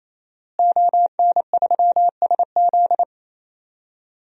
Morse code, heard as a keyed tone, about 28 wpm, 700 Hz. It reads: ON3SZ
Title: DXCC-100 Call Signs - 5 Characters - Repeated 3x - 28wpm